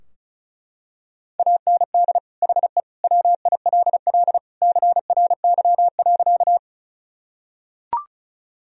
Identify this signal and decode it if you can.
Morse code, heard as a keyed tone, about 36 wpm, 700 Hz. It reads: AND HE WILL CRY.
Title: Sentences from Top 1000 Words - Review Part 2 - 35wpm